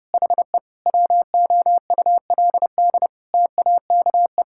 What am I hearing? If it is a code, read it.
Morse code, heard as a keyed tone, about 30 wpm, 700 Hz. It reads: HE WOULD TAKE
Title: Sentences from Top 1000 Words - Review Part 6 - 30wpm